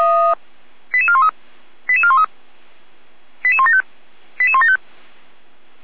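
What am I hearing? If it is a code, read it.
Five-tone selective call (ZVEI), heard as quick runs of five tones, tones of about 70 ms. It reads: 88413 88413 88175 88175
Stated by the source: ZVEI five-tone selective call 88080 received by SDR.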